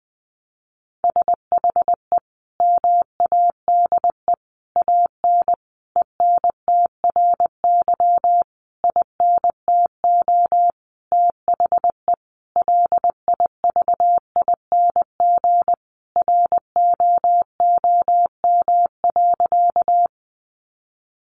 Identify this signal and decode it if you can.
Morse code, heard as a keyed tone, about 20 wpm, 700 Hz. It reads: SHE MADE AN ENTRY INTO THE LIVING ROOM.
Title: Sentences from Top 1500 Words 20wpm